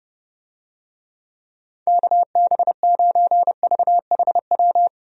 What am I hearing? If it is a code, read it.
Morse code, heard as a keyed tone, about 30 wpm, 700 Hz. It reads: KB9VHW
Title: USA Call Signs - 6 Characters - Repeated 2x - 30wpm